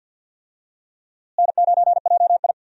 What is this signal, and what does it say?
Morse code, heard as a keyed tone, about 51 wpm, 700 Hz. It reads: N9JI
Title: USA Call Signs - 4 Characters - Repeated 3x - 50wpm